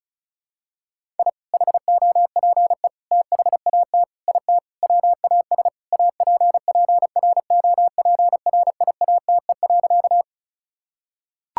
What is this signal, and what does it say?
Morse code, heard as a keyed tone, about 34 wpm, 700 Hz. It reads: I HOPE THAT IT WAS APPROPRIATE.
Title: Sentences from Top 1000 Words - Review Part 5 - 35wpm